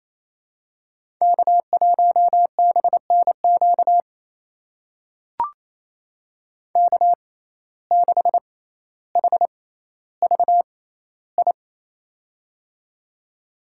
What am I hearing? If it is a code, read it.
Morse code, heard as a keyed tone, about 28 wpm, spaced as for 14 wpm, 700 Hz. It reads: K1BNQ K6HVI
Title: USA Call Signs - 5 Characters - 6x Character Spacing - 28wpm